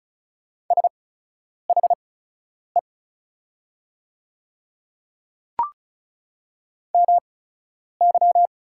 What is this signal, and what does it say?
Morse code, heard as a keyed tone, about 36 wpm, spaced as for 10 wpm, 700 Hz. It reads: SHE MY
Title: 8x Character Spacing - Top 100 Words - 35wpm